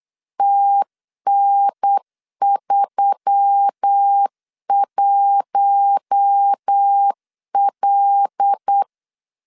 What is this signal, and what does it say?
Morse code, heard as a keyed tone, about 8 wpm, 800 Hz. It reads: TN31L
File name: LEZIONI / 6 PKX0 / 6 PKX0 8wpm 3341